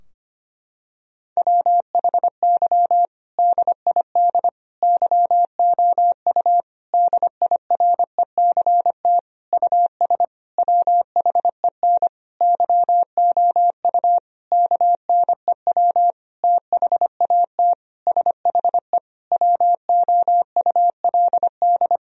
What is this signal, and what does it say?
Morse code, heard as a keyed tone, about 25 wpm, 700 Hz. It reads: WHY DID YOU DIRECT US WHEN YOU KNEW THAT SHE WOULD
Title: Sentences from Top 1000 Words - Review Part 2 - 25wpm